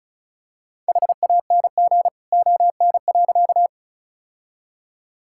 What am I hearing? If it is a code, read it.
Morse code, heard as a keyed tone, about 36 wpm, 700 Hz. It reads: HANG ON.
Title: Sentences from Top 1600 Words 35wpm